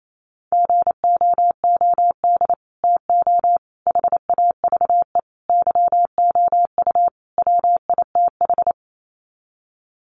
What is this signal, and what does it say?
Morse code, heard as a keyed tone, about 28 wpm, 700 Hz. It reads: GOOD TO HAVE YOU WITH
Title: Sets of 5 Words Encore2 28wpm